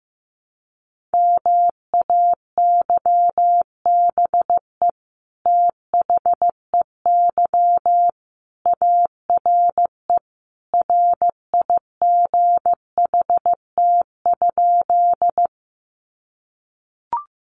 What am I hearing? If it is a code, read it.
Morse code, heard as a keyed tone, about 15 wpm, 700 Hz. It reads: MAYBE THEY ARE RIGHT?
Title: Sentences from Top 1000 Words 15wpm